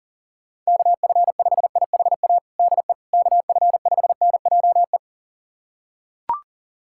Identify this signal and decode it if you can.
Morse code, heard as a keyed tone, about 37 wpm, 700 Hz. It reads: KF5IHA DE KF5NJE
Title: USA Callsign DE Callsign 40wpm